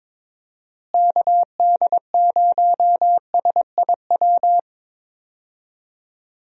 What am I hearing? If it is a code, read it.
Morse code, heard as a keyed tone, about 22 wpm, 700 Hz. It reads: KD0SIW
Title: USA Call Signs - 6 Characters - Repeated 2x - 22wpm